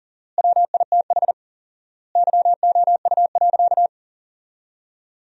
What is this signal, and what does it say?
Morse code, heard as a keyed tone, about 40 wpm, 700 Hz. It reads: WITH YOU.
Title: 4x Word-Spacing - Sentences from Top 300 Words - 40wpm